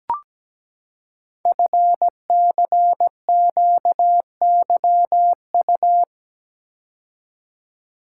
Morse code, heard as a keyed tone, about 17 wpm, 700 Hz. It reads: FCQYU
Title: Code Groups - 5 Letters - 17wpm